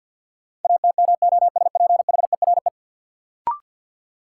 Morse code, heard as a keyed tone, about 50 wpm, 700 Hz. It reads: ATMOSPHERE
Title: Sentences from Top 1600 Words - New Word Drill - 50wpm